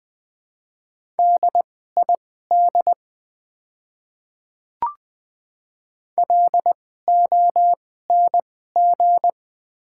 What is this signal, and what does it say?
Morse code, heard as a keyed tone, about 20 wpm, spaced as for 14 wpm, 700 Hz. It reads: DID LONG
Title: 2x Character Spacing - Top 200 Words - 20wpm